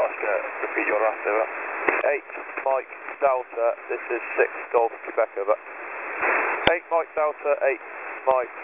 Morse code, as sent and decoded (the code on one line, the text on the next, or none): none